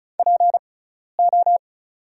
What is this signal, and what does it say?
Morse code, heard as a keyed tone, about 34 wpm, spaced as for 12 wpm, 700 Hz. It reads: PO
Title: 6x Character Spacing - Top 500 Words - 35wpm